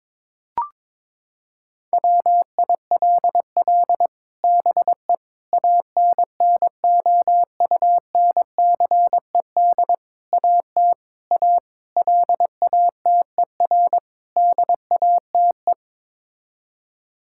Morse code, heard as a keyed tone, about 22 wpm, 700 Hz. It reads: WILL BE ANNOUNCED AT A LATER DATE
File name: practice / 22wpm / Sets of 7 Words - 22wpm